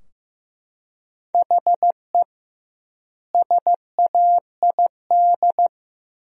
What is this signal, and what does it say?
Morse code, heard as a keyed tone, about 15 wpm, 700 Hz. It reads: HE SAID
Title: 2x Word-Spacing - Sentences from Top 300 Words - 15wpm